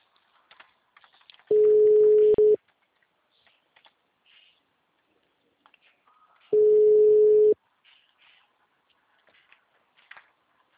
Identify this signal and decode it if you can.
Ringback tone (Europe, 425 Hz).